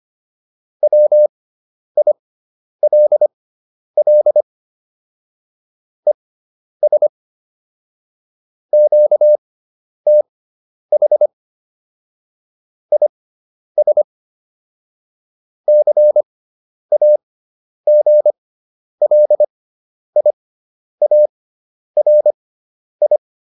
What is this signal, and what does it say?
Morse code, heard as a keyed tone, about 25 wpm, spaced as for 10 wpm, 600 Hz. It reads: WILL ES QTH IS CAGLIARI
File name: QSO409 10